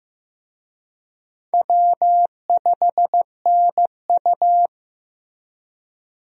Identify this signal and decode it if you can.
Morse code, heard as a keyed tone, about 15 wpm, 700 Hz. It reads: W5NU
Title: USA Call Signs - 4 Characters - Repeated 2x - 15wpm